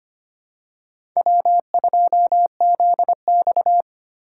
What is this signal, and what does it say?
Morse code, heard as a keyed tone, about 25 wpm, 700 Hz. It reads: W2ZX
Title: USA Call Signs - 4 Characters - 4x Character Spacing - 25wpm